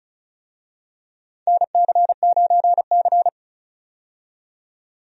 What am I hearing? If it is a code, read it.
Morse code, heard as a keyed tone, about 34 wpm, 700 Hz. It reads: NC9C
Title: USA Call Signs - 4 Characters - 35wpm